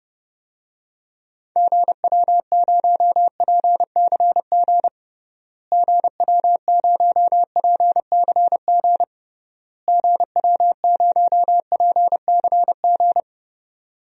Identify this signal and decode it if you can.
Morse code, heard as a keyed tone, about 32 wpm, 700 Hz. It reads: GW0PCG GW0PCG GW0PCG
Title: DXCC-100 Call Signs - 6 Characters - Repeated 3x - 30wpm